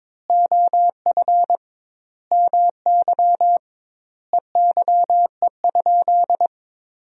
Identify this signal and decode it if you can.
Morse code, heard as a keyed tone, about 22 wpm, 700 Hz. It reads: OF MY EYE?
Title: 2x Word-Spacing - Sentences from Top 500 Words - 22wpm